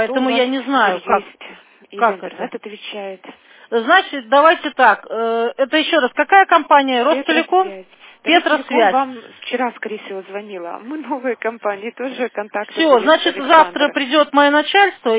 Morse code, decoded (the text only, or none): none